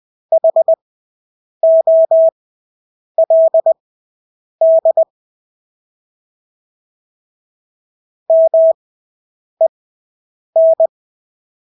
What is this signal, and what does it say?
Morse code, heard as a keyed tone, about 20 wpm, spaced as for 8 wpm, 650 Hz. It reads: HOLD MEN
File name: Engelske ord 20/8 wpm